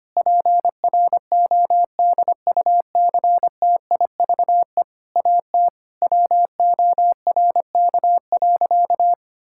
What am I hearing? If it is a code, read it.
Morse code, heard as a keyed tone, about 25 wpm, 700 Hz. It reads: PRODUCTIVE AT WORK.
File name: practice / 25wpm / Difficult Sentences with Hints - 25wpm